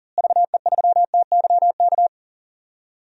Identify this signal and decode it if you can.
Morse code, heard as a keyed tone, about 40 wpm, 700 Hz. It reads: VE3TYK
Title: DXCC-100 Call Signs - 6 Characters - Repeated 3x - 40wpm